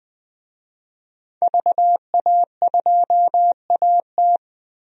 Morse code, heard as a keyed tone, about 20 wpm, 700 Hz. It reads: VA2AT
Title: DXCC-100 Call Signs - 5 Characters - Repeated 2x - 20wpm